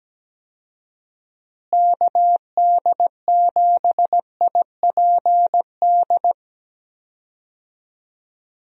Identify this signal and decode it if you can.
Morse code, heard as a keyed tone, about 17 wpm, 700 Hz. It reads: KD7IPD
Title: USA Call Signs - 6 Characters - Repeated 2x - 17wpm